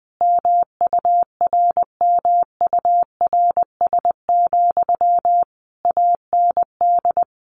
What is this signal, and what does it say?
Morse code, heard as a keyed tone, about 20 wpm, 700 Hz. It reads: MURMURS, AND
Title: A Sound Of Thunder 20wpm